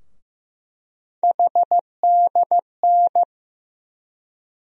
Morse code, heard as a keyed tone, about 15 wpm, 700 Hz. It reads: HDN
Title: USA Call Sign Postfixes - 3 Letters 15wpm